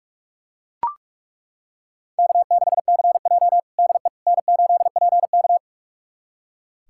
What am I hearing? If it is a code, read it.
Morse code, heard as a keyed tone, about 42 wpm, 700 Hz. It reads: K6CJ DE N8PK
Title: USA Callsign DE Callsign - 4 Characters - 45wpm